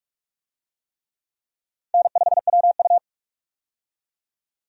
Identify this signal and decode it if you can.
Morse code, heard as a keyed tone, about 46 wpm, 700 Hz. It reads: N5WU
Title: USA Call Signs - 4 Characters - 45wpm